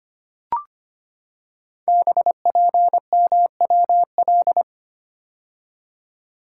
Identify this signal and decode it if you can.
Morse code, heard as a keyed tone, about 25 wpm, 700 Hz. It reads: BPMWL
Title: Code Groups - 5 Letters - 25wpm